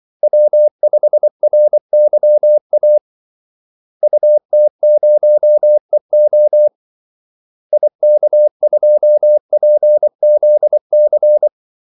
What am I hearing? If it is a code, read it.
Morse code, heard as a keyed tone, about 24 wpm, 600 Hz. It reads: W5RYA UT0EO IK2PZC
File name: Callsigns-24wpm